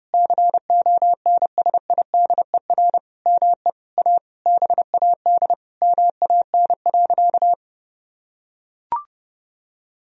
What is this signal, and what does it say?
Morse code, heard as a keyed tone, about 32 wpm, 700 Hz. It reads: CONSIDER ME A BAD MAN.
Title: Sentences from Top 1000 Words - Review Part 4 - 30wpm